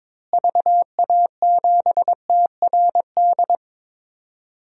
Morse code, heard as a keyed tone, about 22 wpm, 700 Hz. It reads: VA7TRD
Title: DXCC-100 Call Signs - 6 Characters - Repeated 2x - 22wpm